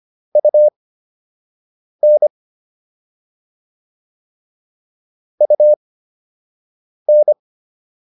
Morse code, heard as a keyed tone, about 25 wpm, spaced as for 6 wpm, 600 Hz. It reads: UN UN
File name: UN5 06